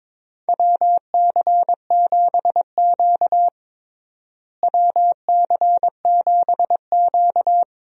Morse code, heard as a keyed tone, about 22 wpm, 700 Hz. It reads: WC7Q WC7Q
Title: USA Call Signs - 4 Characters - Repeated 2x - 22wpm